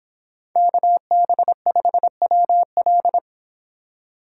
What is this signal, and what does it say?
Morse code, heard as a keyed tone, about 25 wpm, 700 Hz. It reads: KB5WL
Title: Speed Racing - USA 2x2 Call Signs 22wpm